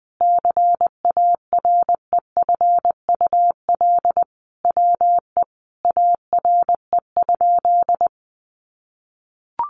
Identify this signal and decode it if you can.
Morse code, heard as a keyed tone, about 20 wpm, 700 Hz. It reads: CAREFUL WE ARE?
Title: A Sound Of Thunder 20wpm